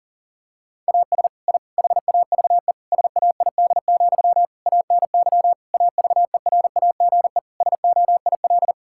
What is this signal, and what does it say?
Morse code, heard as a keyed tone, about 40 wpm, 700 Hz. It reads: AS I HAVE SAID, ANY AVERAGE SOIL
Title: Sentences from Top 1000 Words - Review Part 6 - 40wpm